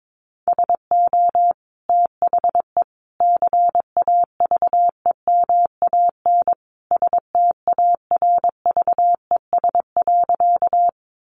Morse code, heard as a keyed tone, about 22 wpm, 700 Hz. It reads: SO THE CAVEMAN STARVES.
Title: A Sound Of Thunder 22wpm